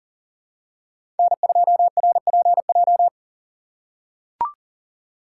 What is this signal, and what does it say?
Morse code, heard as a keyed tone, about 40 wpm, 700 Hz. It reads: N2RPJ
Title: USA Call Signs - 5 Characters - 6x Character Spacing - 40wpm